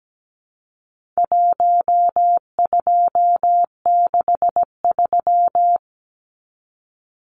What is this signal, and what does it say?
Morse code, heard as a keyed tone, about 17 wpm, 700 Hz. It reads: 1263
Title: Numbers - 4-Digit 17wpm